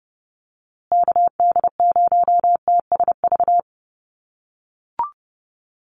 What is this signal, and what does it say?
Morse code, heard as a keyed tone, about 30 wpm, 700 Hz. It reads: KD0TSV
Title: USA Call Signs Repeated 2x 30wpm